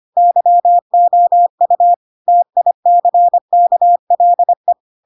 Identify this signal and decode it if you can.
Morse code, heard as a keyed tone, about 25 wpm, 700 Hz. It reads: YOU TICKLE
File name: Pinocchio / Pinocchio ch1 25 25 ik1hge